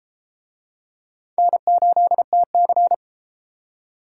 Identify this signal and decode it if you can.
Morse code, heard as a keyed tone, about 32 wpm, 700 Hz. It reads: N8TC